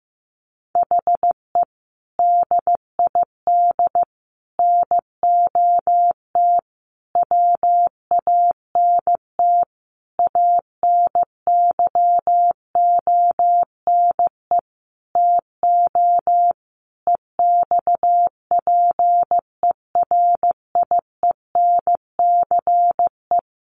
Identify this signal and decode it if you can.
Morse code, heard as a keyed tone, about 15 wpm, 700 Hz. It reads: HE DID NOT WANT ANYONE TO EXPERIENCE
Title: Sentences from Top 1100 Words 15wpm